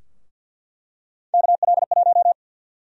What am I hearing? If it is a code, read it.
Morse code, heard as a keyed tone, about 52 wpm, 700 Hz. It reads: KL1
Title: USA Call Sign Prefixes - Repeated 2x - 50wpm